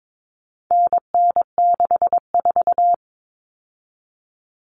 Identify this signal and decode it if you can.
Morse code, heard as a keyed tone, about 22 wpm, 700 Hz. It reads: NN64
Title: Common Cut Numbers - 4-Digit 22wpm